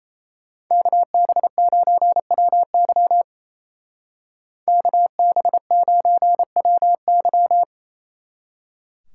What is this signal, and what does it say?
Morse code, heard as a keyed tone, about 37 wpm, 700 Hz. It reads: KB9WY KB9WY